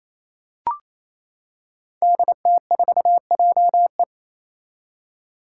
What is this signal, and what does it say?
Morse code, heard as a keyed tone, about 27 wpm, 700 Hz. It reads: DT4JE
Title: Five Challenging Letter-Number Combinations 28wpm